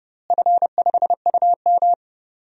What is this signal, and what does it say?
Morse code, heard as a keyed tone, about 31 wpm, 700 Hz. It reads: F5UM
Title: DXCC-100 Call Signs - 4 Characters - Repeated 3x - 30wpm